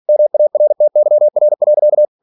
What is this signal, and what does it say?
Morse code, heard as a keyed tone, about 46 wpm, 600 Hz. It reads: MARTYR.
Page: Never 47wpm